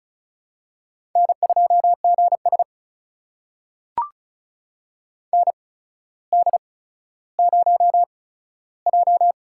Morse code, heard as a keyed tone, about 36 wpm, spaced as for 15 wpm, 700 Hz. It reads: N2GS ND0J
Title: USA Call Signs - 4 Characters - 8x Character Spacing - 35wpm